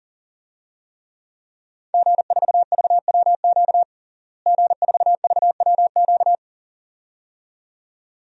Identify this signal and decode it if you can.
Morse code, heard as a keyed tone, about 40 wpm, 700 Hz. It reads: G4VWQ G4VWQ
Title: DXCC-100 Call Signs - 5 Characters - Repeated 2x - 40wpm